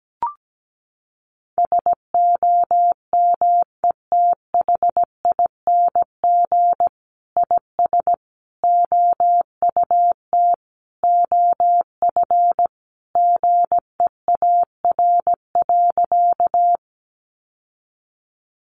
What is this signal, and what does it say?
Morse code, heard as a keyed tone, about 17 wpm, 700 Hz. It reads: SOMETHING IS OUT OF GEAR.
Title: Sentences from Top 1400 Words 17wpm